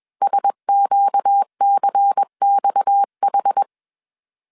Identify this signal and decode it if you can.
Morse code, heard as a keyed tone, about 21 wpm, 800 Hz. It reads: SQCXH
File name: CW / DET / 1 6 1500C S C 20wpm 2116